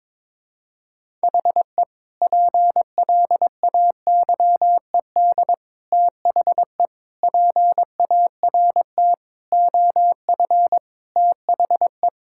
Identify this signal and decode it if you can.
Morse code, heard as a keyed tone, about 22 wpm, 700 Hz. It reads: HE PLAYED THE PART OF THE
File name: practice / 22wpm / Sentences with 4-Word Phrases - 22wpm